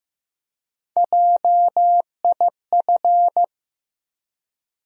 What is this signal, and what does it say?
Morse code, heard as a keyed tone, about 16 wpm, 700 Hz. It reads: JIF
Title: USA Call Sign Postfixes - 3 Letters 15wpm